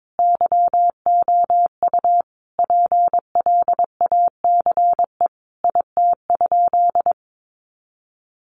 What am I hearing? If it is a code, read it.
Morse code, heard as a keyed tone, about 22 wpm, 700 Hz. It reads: YOU PLACE IT?
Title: Sentences from Top 200 Words 22wpm